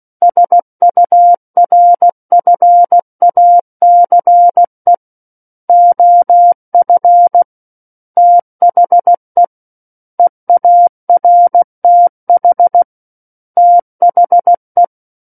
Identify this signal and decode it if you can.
Morse code, heard as a keyed tone, about 16 wpm, 700 Hz. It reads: SURFACE OF THE EARTH THE
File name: Book One / 07-Chapter 7 - 16 WPM